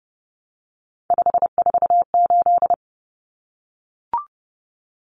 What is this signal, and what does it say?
Morse code, heard as a keyed tone, about 30 wpm, 700 Hz. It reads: 548
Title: Numbers - 3-Digit 30wpm